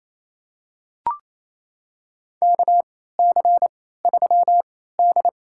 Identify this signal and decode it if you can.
Morse code, heard as a keyed tone, about 28 wpm, spaced as for 16 wpm, 700 Hz. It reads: KC3D
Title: USA Call Signs - 3x Character Spacing - 28wpm